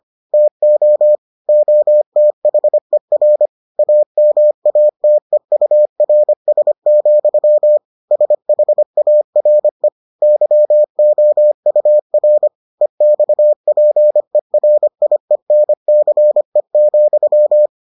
Morse code, heard as a keyed tone, about 25 wpm, 600 Hz. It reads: TO OTHER AMATEURS, SHARE YOUR EXPERIENCE,